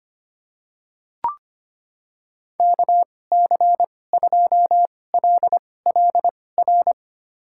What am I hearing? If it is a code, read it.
Morse code, heard as a keyed tone, about 25 wpm, spaced as for 18 wpm, 700 Hz. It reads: KC2LLR
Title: USA Call Signs - 6 Characters - 2x Character Spacing - 25wpm